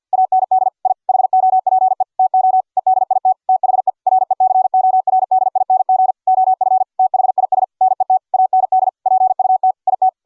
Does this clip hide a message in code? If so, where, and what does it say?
Yes: Morse code from 0.1 to 10.1 s, heard as a keyed tone, about 48 wpm, 750 Hz. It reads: AND I HOPE TO EDIT THE RECORDING OF THIS NET AND PUT IT